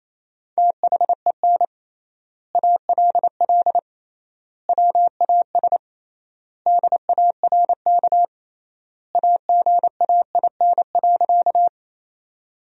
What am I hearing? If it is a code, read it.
Morse code, heard as a keyed tone, about 28 wpm, 700 Hz. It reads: THEN ALL WAS DARK AGAIN.
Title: 3x Word-Spacing - Sentences from Top 500 Words - 28wpm